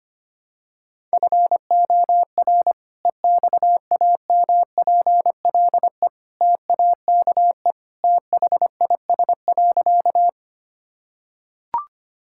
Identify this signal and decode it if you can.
Morse code, heard as a keyed tone, about 25 wpm, 700 Hz. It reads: FOR EXAMPLE TAKE THIS.
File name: practice / 25wpm / Sentences with 2-Word Phrases - 25wpm